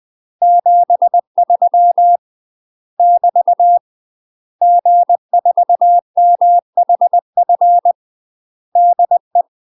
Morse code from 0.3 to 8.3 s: 73 <BT> G4MHF